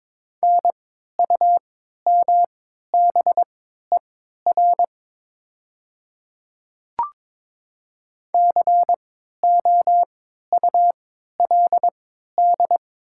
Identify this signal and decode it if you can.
Morse code, heard as a keyed tone, about 22 wpm, spaced as for 12 wpm, 700 Hz. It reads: NUMBER COULD
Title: 3x Character Spacing - Top 200 Words - 22wpm